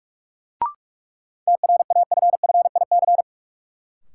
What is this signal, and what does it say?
Morse code, heard as a keyed tone, about 41 wpm, 700 Hz. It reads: TRAFFIC
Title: Sentences from Top 1300 Words - New Word Drill - 45wpm